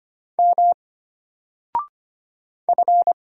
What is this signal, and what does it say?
Morse code, heard as a keyed tone, about 25 wpm, 700 Hz. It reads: M F
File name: Single-Letter-Number-25wpm